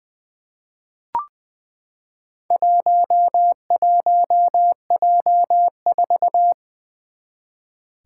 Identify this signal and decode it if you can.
Morse code, heard as a keyed tone, about 20 wpm, 700 Hz. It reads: 11J4
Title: Four Challenging Letter-Number Combinations 20wpm